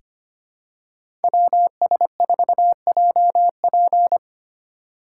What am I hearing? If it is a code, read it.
Morse code, heard as a keyed tone, about 25 wpm, 700 Hz. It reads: WS4JP
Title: USA Call Signs - 5 Characters - 6x Character Spacing - 25wpm